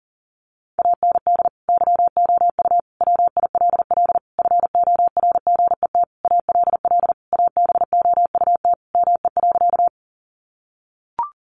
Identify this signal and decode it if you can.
Morse code, heard as a keyed tone, about 37 wpm, 700 Hz. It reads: AND YOU WILL FORGET ALL ABOUT ME.
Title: Sentences from Top 1200 Words 40wpm